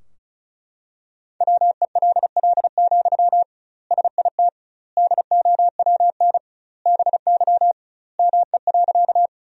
Morse code, heard as a keyed tone, about 34 wpm, 700 Hz. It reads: WELL, SIT DOWN BY ME.
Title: 2x Word-Spacing - Sentences from Top 500 Words - 35wpm